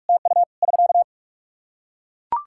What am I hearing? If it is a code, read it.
Morse code, heard as a keyed tone, about 41 wpm, 700 Hz. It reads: TU <SK>
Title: Practice QSO - Exchange 4 - Ending - 45wpm